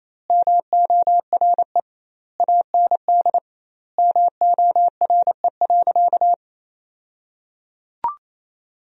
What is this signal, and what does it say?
Morse code, heard as a keyed tone, about 28 wpm, 700 Hz. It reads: MORE AND MORE.
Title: 2x Word-Spacing - Sentences from Top 500 Words - 28wpm